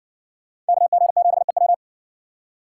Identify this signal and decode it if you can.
Morse code, heard as a keyed tone, about 62 wpm, 700 Hz. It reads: KG7EW